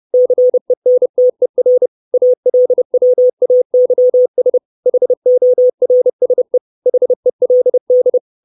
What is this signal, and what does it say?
Morse code, heard as a keyed tone, about 30 wpm, 500 Hz. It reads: CENTER ALWAYS HORSE HELD
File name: 632 Words 30WPM